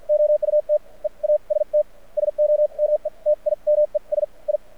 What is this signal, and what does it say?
Morse code, heard as a keyed tone, about 53 wpm, spaced as for 40 wpm, 600 Hz. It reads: OUT EAST SOMETIMES I
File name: archived / CRSnet